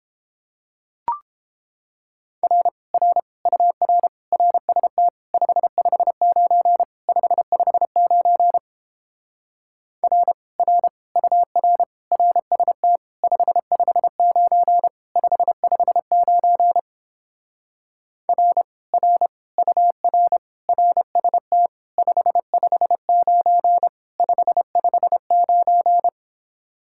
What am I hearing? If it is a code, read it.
Morse code, heard as a keyed tone, about 29 wpm, 700 Hz. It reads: R R UR RST 559 559 R R UR RST 559 559 R R UR RST 559 559